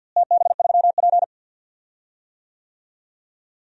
Morse code, heard as a keyed tone, about 52 wpm, 700 Hz. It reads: TB3P
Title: DXCC-100 Call Signs - 4 Characters - Repeated 3x - 50wpm